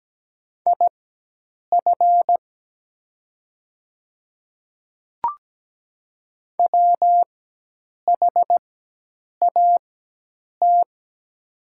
Morse code, heard as a keyed tone, about 17 wpm, spaced as for 8 wpm, 700 Hz. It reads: IF WHAT
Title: 4x Character Spacing - Top 100 Words - 17wpm